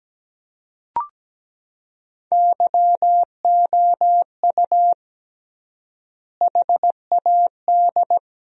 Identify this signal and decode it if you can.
Morse code, heard as a keyed tone, about 17 wpm, 700 Hz. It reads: YOU HAD
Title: 3x Word-Spacing - Sentences from Top 400 Words - 17wpm